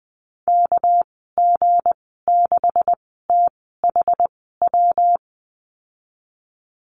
Morse code, heard as a keyed tone, about 20 wpm, spaced as for 14 wpm, 700 Hz. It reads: KG6THW
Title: 238 Lesson 38g - Letter q - Call Signs 20wpm